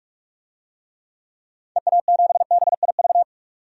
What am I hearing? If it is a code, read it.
Morse code, heard as a keyed tone, about 41 wpm, 700 Hz. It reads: EA7BIV
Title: DXCC-100 Call Signs - 6 Characters - 45wpm